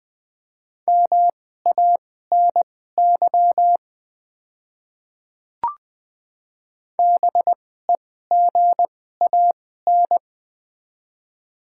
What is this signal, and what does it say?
Morse code, heard as a keyed tone, about 21 wpm, spaced as for 15 wpm, 690 Hz. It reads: MANY BEGAN
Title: 2x Character Spacing - Top 500 Words - 20wpm